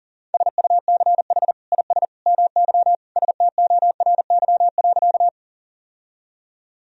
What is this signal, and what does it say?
Morse code, heard as a keyed tone, about 40 wpm, 700 Hz. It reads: SUCH IS MY STORY.
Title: Sentences from Top 1000 Words - Review Part 1 - 40wpm